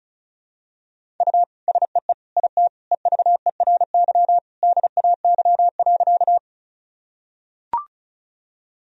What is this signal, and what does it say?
Morse code, heard as a keyed tone, about 34 wpm, 690 Hz. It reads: U SEE IT EVERY DAY.
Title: Sentences from Top 200 Words and Abbreviations 35wpm